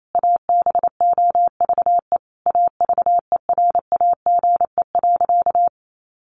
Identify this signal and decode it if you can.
Morse code, heard as a keyed tone, about 28 wpm, 700 Hz. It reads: ABOVE AVERAGE.
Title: Sentences from Top 1400 Words 28wpm